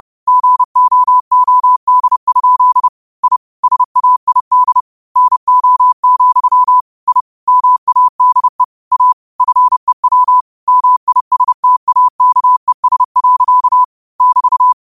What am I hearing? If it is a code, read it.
Morse code, heard as a keyed tone, about 30 wpm, 1000 Hz. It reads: GOOD? I SAID NO, I MADE A FEW MISTAKES. <BT>
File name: qotdfiles / 30wpm